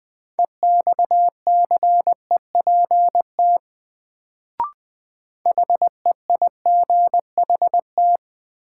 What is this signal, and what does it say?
Morse code, heard as a keyed tone, about 20 wpm, 700 Hz. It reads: EXCEPT HEIGHT
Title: Sentences from Top 1600 Words - New Word Drill - 20wpm